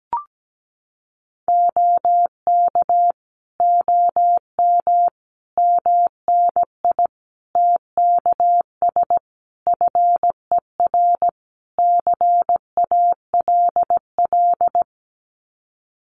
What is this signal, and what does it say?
Morse code, heard as a keyed tone, about 17 wpm, 700 Hz. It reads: OK OM MNI TKS FER CALL
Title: Common QSO Phrases 17wpm